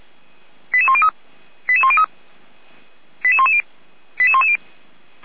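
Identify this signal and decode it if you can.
Five-tone selective call (ZVEI), heard as quick runs of five tones, tones of about 70 ms. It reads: 88103 88103 88110 88110
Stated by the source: Five-tone ZVEI selective call 88104, received on an SDR.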